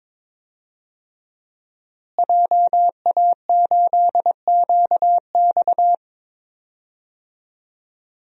Morse code, heard as a keyed tone, about 21 wpm, 700 Hz. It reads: JA8QX
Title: DXCC-100 Call Signs - 5 Characters - Repeated 2x - 22wpm